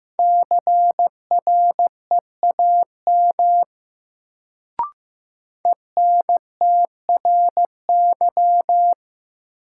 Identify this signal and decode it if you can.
Morse code, heard as a keyed tone, about 15 wpm, 700 Hz. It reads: CREAM ENTRY
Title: Sentences from Top 1500 Words - New Word Drill Alt - 15wpm